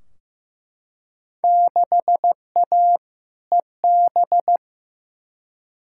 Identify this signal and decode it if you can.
Morse code, heard as a keyed tone, about 15 wpm, 700 Hz. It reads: 6A EB
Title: Field Day Exchanges Encore - 15wpm